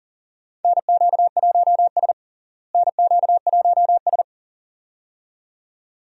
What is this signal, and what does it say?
Morse code, heard as a keyed tone, about 37 wpm, 700 Hz. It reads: NQ1S NQ1S
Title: USA Call Signs - 4 Characters - Repeated 2x - 40wpm